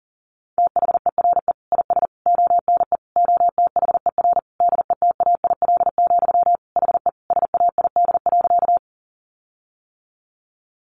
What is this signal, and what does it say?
Morse code, heard as a keyed tone, about 37 wpm, 700 Hz. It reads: THERE IS ONE OTHER DETAIL, HE SAID.
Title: Sentences from Top 1500 Words 40wpm